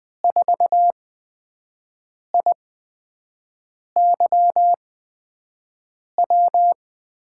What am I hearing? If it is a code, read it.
Morse code, heard as a keyed tone, about 20 wpm, spaced as for 5 wpm, 700 Hz. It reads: 4IYW
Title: USA Call Signs - 6 Characters - 8x Character Spacing - 20wpm